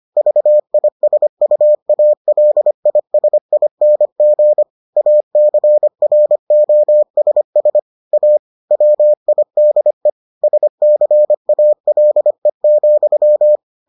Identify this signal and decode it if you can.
Morse code, heard as a keyed tone, about 25 wpm, 600 Hz. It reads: VISUALISING ACROSS A WIDE SCALE,